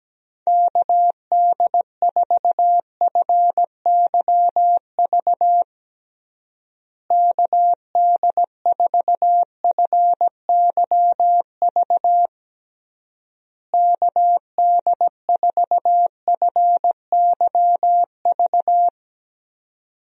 Morse code, heard as a keyed tone, about 17 wpm, 700 Hz. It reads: KD4FYV KD4FYV KD4FYV
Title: USA Call Signs - 6 Characters - Repeated 3x - 17wpm